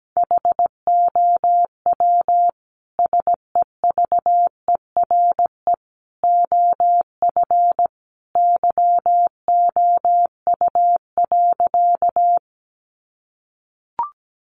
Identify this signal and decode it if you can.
Morse code, heard as a keyed tone, about 17 wpm, 700 Hz. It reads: HOW SEVERE OF YOU.
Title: Sentences from Top 1400 Words 17wpm